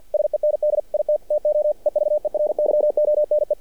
Morse code, heard as a keyed tone, about 50 wpm, 600 Hz. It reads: SEND IT TO EVERYONE
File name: archived / CRSnet